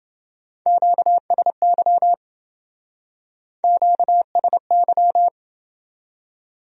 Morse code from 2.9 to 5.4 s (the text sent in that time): QSY